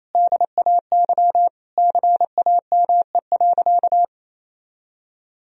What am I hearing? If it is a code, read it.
Morse code, heard as a keyed tone, about 28 wpm, 700 Hz. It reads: DAY CAME.
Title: Sentences from Top 1000 Words - Review Part 2 - 28wpm